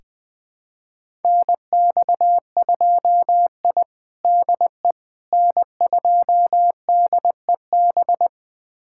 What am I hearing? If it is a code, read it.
Morse code, heard as a keyed tone, about 20 wpm, 700 Hz. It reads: NX2I DE N2DEB
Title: USA Callsign DE Callsign 20wpm